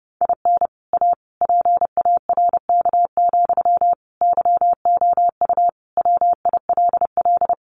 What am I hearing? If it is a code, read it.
Morse code, heard as a keyed tone, about 30 wpm, 700 Hz. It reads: IN A PARK, YOU WILL
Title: Sentences from Top 1300 Words 30wpm